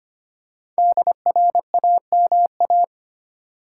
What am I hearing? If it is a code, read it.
Morse code, heard as a keyed tone, about 26 wpm, 700 Hz. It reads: DRAMA
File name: 5 Letter Words - First 3 Letters Given - 25wpm